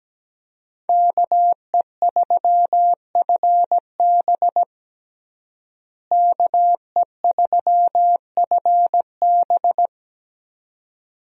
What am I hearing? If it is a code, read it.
Morse code, heard as a keyed tone, about 17 wpm, 700 Hz. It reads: KE3FB KE3FB
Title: USA Call Signs - 5 Characters - Repeated 2x - 17wpm